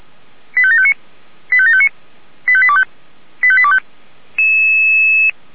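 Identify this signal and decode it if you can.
Five-tone selective call (ZVEI), heard as quick runs of five tones, tones of about 70 ms. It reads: 85759 85759 85825 85825
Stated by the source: ZVEI five-tone selective call received on an SDR.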